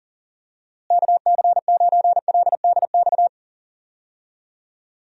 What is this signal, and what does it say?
Morse code, heard as a keyed tone, about 40 wpm, 700 Hz. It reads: KC9LDX
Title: USA Call Signs - 6 Characters - Repeated 3x - 40wpm